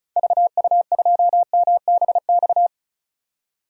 Morse code, heard as a keyed tone, about 36 wpm, 700 Hz. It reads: VU2MBX
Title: DXCC-100 Call Signs - 6 Characters - 35wpm